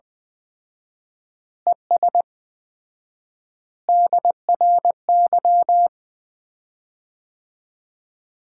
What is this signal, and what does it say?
Morse code, heard as a keyed tone, about 20 wpm, 700 Hz. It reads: ES DRY
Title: Common QSO Phrases - 4x Word-Spacing - 20wpm